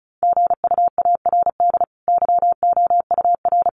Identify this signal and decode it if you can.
Morse code, heard as a keyed tone, about 34 wpm, 700 Hz. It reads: GUARD YOUR
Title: Sentences from Top 1200 Words 35wpm